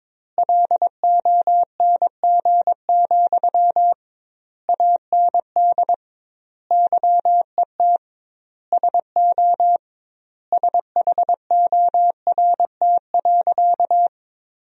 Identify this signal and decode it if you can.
Morse code, heard as a keyed tone, about 22 wpm, 700 Hz. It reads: LONG, AND YET SO SHORT.
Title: 2x Word-Spacing - Sentences from Top 500 Words - 22wpm